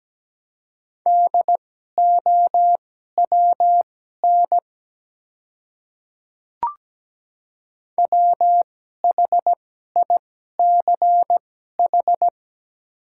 Morse code, heard as a keyed tone, about 17 wpm, spaced as for 12 wpm, 700 Hz. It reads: DOWN WHICH
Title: 2x Character Spacing - Top 100 Words - 17wpm